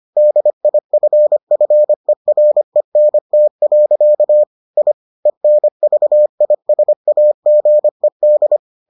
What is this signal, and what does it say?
Morse code, heard as a keyed tone, about 25 wpm, 600 Hz. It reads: DIFFERENT. I ENVISAGED